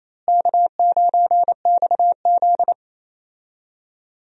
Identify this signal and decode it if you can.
Morse code, heard as a keyed tone, about 27 wpm, 700 Hz. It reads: K9XZ
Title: USA Call Signs - 4 Characters - Repeated 3x - 28wpm